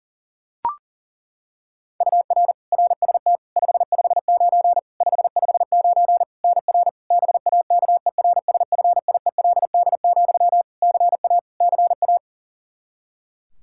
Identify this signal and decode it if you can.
Morse code, heard as a keyed tone, about 40 wpm, 700 Hz. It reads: UR RST 559 559 NR BAKERSFIELD, CA CA
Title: Practice QSO - Exchange 1 - Beginning - 40wpm